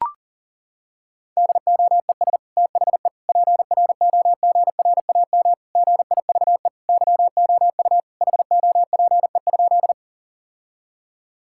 Morse code, heard as a keyed tone, about 40 wpm, 700 Hz. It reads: DOES THE PROGRAM GIVE YOU HOPE?
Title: Sentences from Top 1000 Words - Review Part 5 - 40wpm